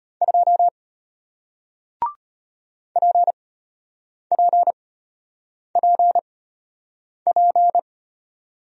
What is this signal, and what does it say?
Morse code, heard as a keyed tone, about 33 wpm, spaced as for 8 wpm, 700 Hz. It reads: 2 PPPP